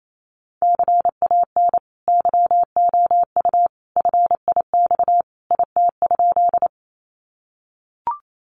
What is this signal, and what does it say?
Morse code, heard as a keyed tone, about 28 wpm, 700 Hz. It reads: CAN YOU FIX IT?
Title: Sentences from Top 1400 Words 28wpm